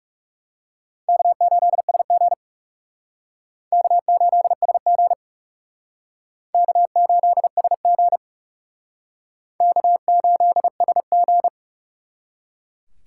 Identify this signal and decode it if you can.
Morse code, heard as a keyed tone, about 37 wpm, 700 Hz. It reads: K8SG K8SG K8SG K8SG
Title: Speed Racing - USA 1x2 Call Signs 30wpm